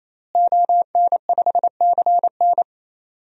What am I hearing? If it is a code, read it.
Morse code, heard as a keyed tone, about 28 wpm, 700 Hz. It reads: ON5CN
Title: DXCC-100 Call Signs - 5 Characters - Repeated 3x - 28wpm